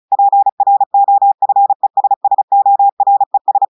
Morse code, heard as a keyed tone, about 35 wpm, 800 Hz. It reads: PROFESSORES
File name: audios / portugais / portugais texte en clair 1 vitesse 35 mots minutes